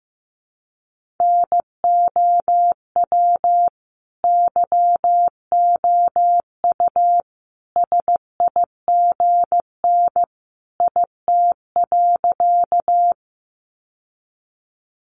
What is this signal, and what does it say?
Morse code, heard as a keyed tone, about 15 wpm, 700 Hz. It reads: NOW YOU SIGN IT.
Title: Sentences from Top 1000 Words 15wpm